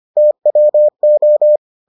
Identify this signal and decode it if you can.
Morse code, heard as a keyed tone, about 25 wpm, 600 Hz. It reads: TWO